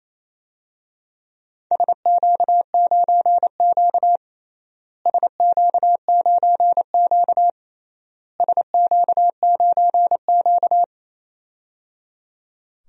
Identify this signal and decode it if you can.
Morse code, heard as a keyed tone, about 28 wpm, 700 Hz. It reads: SQ9Q SQ9Q SQ9Q
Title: DXCC-100 Call Signs - 4 Characters - Repeated 3x - 28wpm